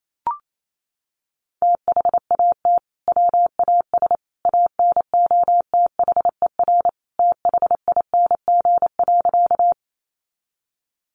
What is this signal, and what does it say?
Morse code, heard as a keyed tone, about 28 wpm, 700 Hz. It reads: THAT WAS ANOTHER THING.
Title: Sentences from Top 600 Words 28wpm